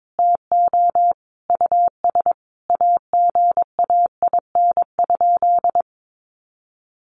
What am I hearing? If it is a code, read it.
Morse code, heard as a keyed tone, about 22 wpm, 700 Hz. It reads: TO US AGAIN?
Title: Sentences from Top 1200 Words 22wpm